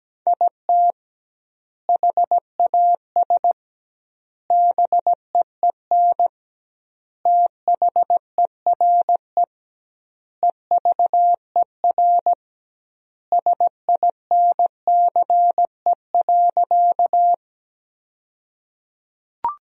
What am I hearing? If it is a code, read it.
Morse code, heard as a keyed tone, about 17 wpm, 700 Hz. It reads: IT HAS BEEN THERE EVER SINCE.
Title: 2x Word-Spacing - Sentences from Top 400 Words - 17wpm